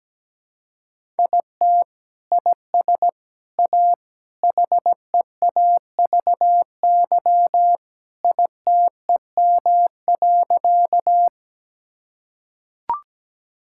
Morse code, heard as a keyed tone, about 17 wpm, 700 Hz. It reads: IT IS A HEAVY ITEM.
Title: Sentences from Top 1000 Words - Review Part 6 - 17wpm